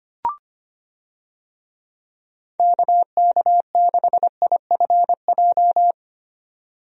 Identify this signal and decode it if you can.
Morse code, heard as a keyed tone, about 25 wpm, 700 Hz. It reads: KK6IFJ
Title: USA Call Signs - 6 Characters - 25wpm